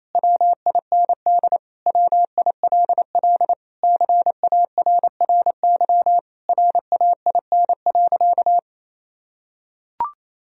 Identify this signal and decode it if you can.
Morse code, heard as a keyed tone, about 28 wpm, 700 Hz. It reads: WIND WILL CARRY RAIN.
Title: Sentences from Top 1000 Words - Review Part 3 - 28wpm